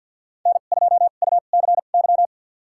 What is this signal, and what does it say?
Morse code, heard as a keyed tone, about 53 wpm, spaced as for 37 wpm, 700 Hz. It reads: N2UCY
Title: USA Call Signs - 2x Character Spacing - 50wpm